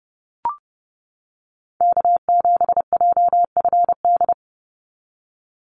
Morse code, heard as a keyed tone, about 32 wpm, 700 Hz. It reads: K7JFD
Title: NA SOTA Chaser Callsigns 30wpm